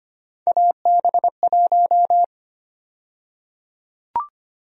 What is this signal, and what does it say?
Morse code, heard as a keyed tone, about 25 wpm, 700 Hz. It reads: AB1
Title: USA Call Sign Prefixes - 25wpm